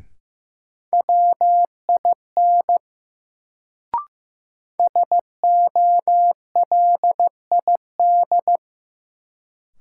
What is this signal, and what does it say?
Morse code, heard as a keyed tone, about 15 wpm, 700 Hz. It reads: WIN SOLID
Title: Sentences from Top 1500 Words - New Word Drill - 15wpm